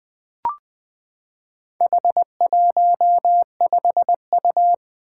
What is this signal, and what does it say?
Morse code, heard as a keyed tone, about 20 wpm, 700 Hz. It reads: H15U
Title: Four Challenging Letter-Number Combinations 20wpm